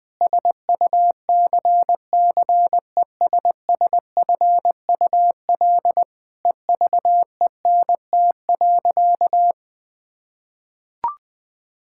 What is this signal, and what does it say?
Morse code, heard as a keyed tone, about 20 wpm, 700 Hz. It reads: SUCCESSFUL EVENT.
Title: Difficult Sentences with Hints - 20wpm